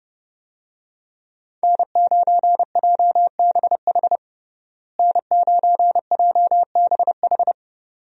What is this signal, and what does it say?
Morse code, heard as a keyed tone, about 32 wpm, 700 Hz. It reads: N9JBH N9JBH
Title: USA Call Signs - 5 Characters - Repeated 2x - 30wpm